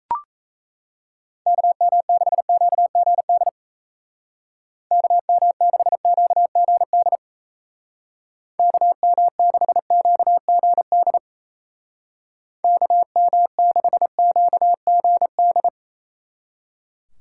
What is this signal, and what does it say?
Morse code, heard as a keyed tone, about 36 wpm, 700 Hz. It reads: KM6QGD KM6QGD KM6QGD KM6QGD